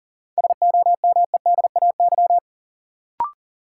Morse code, heard as a keyed tone, about 40 wpm, 700 Hz. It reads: SOMEDAY
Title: Sentences from Top 1600 Words - New Word Drill Alt - 40wpm